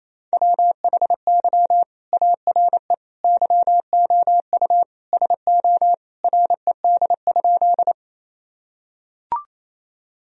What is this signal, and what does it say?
Morse code, heard as a keyed tone, about 28 wpm, 700 Hz. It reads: WHY ARE YOU SO RED?
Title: Sentences from Top 1000 Words - Review Part 2 - 28wpm